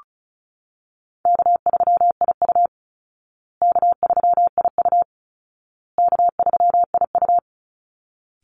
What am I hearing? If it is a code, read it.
Morse code, heard as a keyed tone, about 34 wpm, 700 Hz. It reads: K3IU K3IU K3IU
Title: Easy Call Signs Repeated 3x - 35wpm